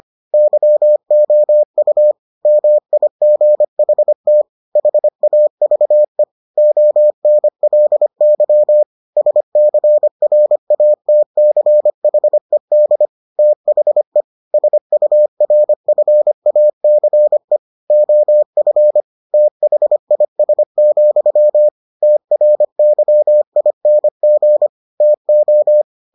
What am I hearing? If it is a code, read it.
Morse code, heard as a keyed tone, about 25 wpm, 600 Hz. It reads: YOU MIGHT HAVE ONLY SCRATCHED THE SURFACE OF THIS, TRYING TO